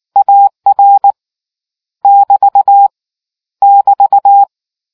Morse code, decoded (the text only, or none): AR <BT> <BT>